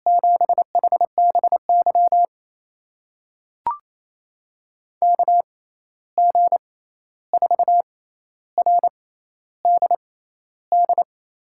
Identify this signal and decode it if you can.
Morse code, heard as a keyed tone, about 28 wpm, spaced as for 13 wpm, 700 Hz. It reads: 7HBY KG4RDD
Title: USA Call Signs - 6 Characters - 6x Character Spacing - 28wpm